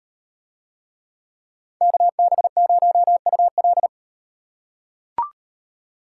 Morse code, heard as a keyed tone, about 36 wpm, 700 Hz. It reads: KB0UL